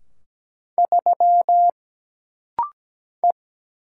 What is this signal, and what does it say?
Morse code, heard as a keyed tone, about 17 wpm, 700 Hz. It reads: 3 E
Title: Single Letter-Number - Rapid-Fire 17wpm